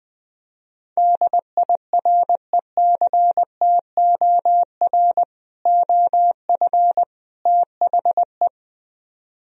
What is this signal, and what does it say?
Morse code, heard as a keyed tone, about 20 wpm, 700 Hz. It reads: DIRECTOR OF THE